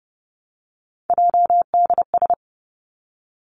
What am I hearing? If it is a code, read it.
Morse code, heard as a keyed tone, about 31 wpm, 700 Hz. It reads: JDS
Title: Three Challenging Letter-Number Combinations 30wpm